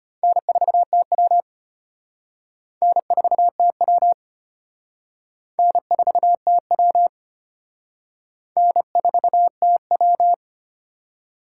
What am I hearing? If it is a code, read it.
Morse code, heard as a keyed tone, about 30 wpm, 700 Hz. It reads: N4TW N4TW N4TW N4TW
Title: Speed Racing - USA 1x2 Call Signs 25wpm